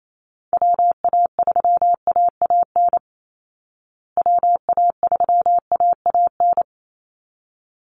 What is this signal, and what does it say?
Morse code, heard as a keyed tone, about 28 wpm, 700 Hz. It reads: WA3AAN WA3AAN
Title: Call Signs Repeated 3x - 28wpm